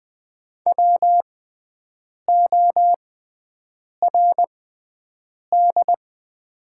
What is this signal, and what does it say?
Morse code, heard as a keyed tone, about 20 wpm, spaced as for 7 wpm, 700 Hz. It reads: WORD
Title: 6x Character Spacing - Top 100 Words - 20wpm